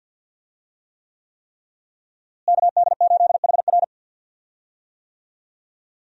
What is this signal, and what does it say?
Morse code, heard as a keyed tone, about 51 wpm, 700 Hz. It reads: KD8HR
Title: USA Call Signs - 5 Characters - Repeated 3x - 50wpm